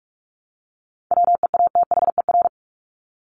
Morse code, heard as a keyed tone, about 44 wpm, 700 Hz. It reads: WEATHER
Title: Top 501-750 Words 45wpm